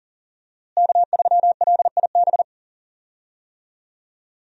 Morse code, heard as a keyed tone, about 40 wpm, 700 Hz. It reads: K3LIB
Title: USA Call Signs - 5 Characters - Repeated 3x - 40wpm